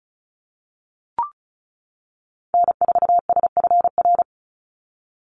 Morse code, heard as a keyed tone, about 34 wpm, 700 Hz. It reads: N4SFR
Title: NA SOTA Activator Callsigns 35wpm